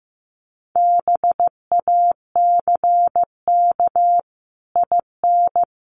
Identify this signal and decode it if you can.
Morse code, heard as a keyed tone, about 14 wpm, 700 Hz. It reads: BACK IN
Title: Sets of 5 Words Encore 15wpm